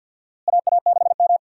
Morse code, heard as a keyed tone, about 48 wpm, 700 Hz. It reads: AA6M
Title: USA Call Signs - 4 Characters - Repeated 3x - 50wpm